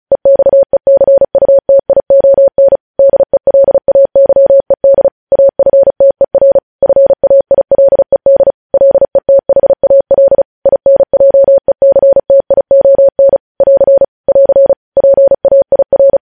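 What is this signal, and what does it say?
Morse code, heard as a keyed tone, about 36 wpm, 550 Hz. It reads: EXECUTION DELAYED AFTER FAILED LETHAL INJECTION <AR> <AR> PAIR
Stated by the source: News headlines sent in Morse code at 35 WPM.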